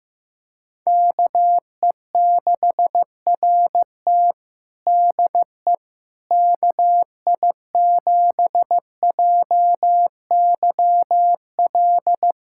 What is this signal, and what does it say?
Morse code, heard as a keyed tone, about 16 wpm, 700 Hz. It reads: KE6RT DE KI7JYL
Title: USA Callsign DE Callsign 15wpm